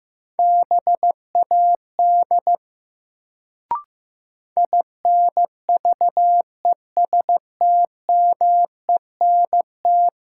Morse code, heard as a keyed tone, about 15 wpm, 700 Hz. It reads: BAD INVESTMENT
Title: Sentences from Top 700 Words - New Word Drill - 15wpm